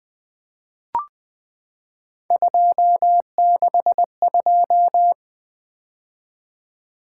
Morse code, heard as a keyed tone, about 20 wpm, 700 Hz. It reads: 262
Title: Three Letter-Number Combinations - 20wpm